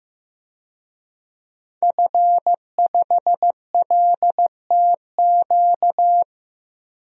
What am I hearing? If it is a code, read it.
Morse code, heard as a keyed tone, about 15 wpm, 700 Hz. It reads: F5LTQ
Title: DXCC-100 Call Signs - 5 Characters - 15wpm